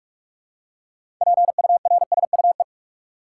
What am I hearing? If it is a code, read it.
Morse code, heard as a keyed tone, about 46 wpm, 700 Hz. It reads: PURSUE
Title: Six Letter Words - 45wpm